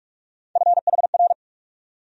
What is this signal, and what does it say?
Morse code, heard as a keyed tone, about 46 wpm, 700 Hz. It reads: FHR
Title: USA Call Sign Postfixes - 3 Letters 45wpm